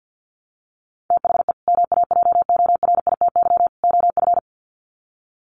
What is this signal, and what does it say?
Morse code, heard as a keyed tone, about 48 wpm, 700 Hz. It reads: THE MAJORITY OF
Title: Sets of 3 Words 50wpm